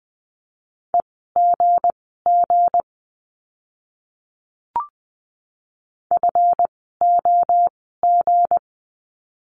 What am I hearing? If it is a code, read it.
Morse code, heard as a keyed tone, about 20 wpm, spaced as for 15 wpm, 700 Hz. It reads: EGG FOG
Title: 160 Lesson 26f - Letter g - Words 20wpm